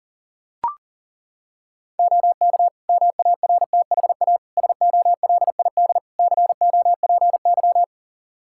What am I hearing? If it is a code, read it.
Morse code, heard as a keyed tone, about 40 wpm, 700 Hz. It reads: OK MARTHA SOLID COPY
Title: Practice QSO - Exchange 3 - More Info - 40wpm